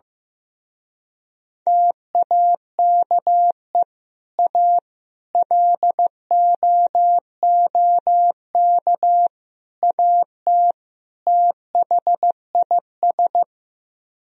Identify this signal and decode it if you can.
Morse code, heard as a keyed tone, about 15 wpm, 700 Hz. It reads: TAKE A LOOK AT THIS
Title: Speed-Racing - Sets of 5 Words 15wpm